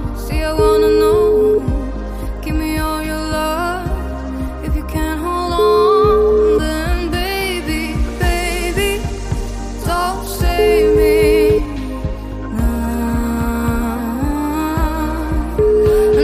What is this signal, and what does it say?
Ringback tone (Europe, 425 Hz), with noise.